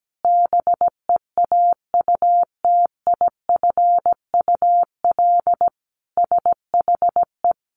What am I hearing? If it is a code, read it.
Morse code, heard as a keyed tone, about 17 wpm, 700 Hz. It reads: BEAUTIFUL SHE
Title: Sentences from Top 900 Words 17wpm